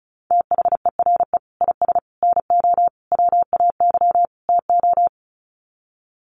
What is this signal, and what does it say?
Morse code, heard as a keyed tone, about 34 wpm, 700 Hz. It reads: THERE IS NO WAY TO
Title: Sets of 5 Words Encore 35wpm